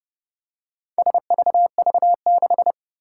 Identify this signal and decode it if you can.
Morse code, heard as a keyed tone, about 30 wpm, 700 Hz. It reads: SVV6
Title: Four Challenging Letter-Number Combinations 30wpm